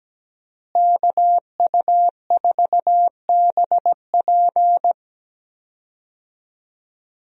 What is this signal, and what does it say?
Morse code, heard as a keyed tone, about 17 wpm, 700 Hz. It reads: KU4BP
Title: USA Call Signs - 5 Characters - Repeated 2x - 17wpm